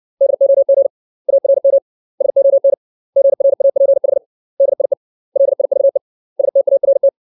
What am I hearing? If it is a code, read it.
Morse code, heard as a keyed tone, about 60 wpm, 550 Hz. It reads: DOG ARM SON GRAPH DIE LIVE START
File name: 632 words / 632 Words 60WPM